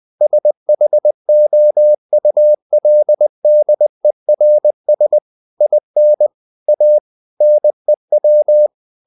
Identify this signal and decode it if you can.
Morse code, heard as a keyed tone, about 20 wpm, 600 Hz. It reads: SHOULDERS IN A NEW
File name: the old man and the sea part 2 20wpm